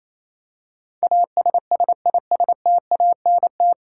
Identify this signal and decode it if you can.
Morse code, heard as a keyed tone, about 28 wpm, 700 Hz. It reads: ASSISTANT
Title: Practice QSO - Exchange 3 - More Info - 28wpm